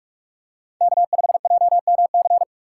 Morse code, heard as a keyed tone, about 46 wpm, 700 Hz. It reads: K5JMC
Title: USA Call Signs - 5 Characters - 2x Character Spacing - 45wpm